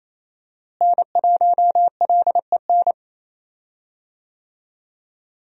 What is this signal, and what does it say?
Morse code, heard as a keyed tone, about 28 wpm, 700 Hz. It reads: N1LEN
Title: USA Call Signs - 5 Characters - Repeated 3x - 28wpm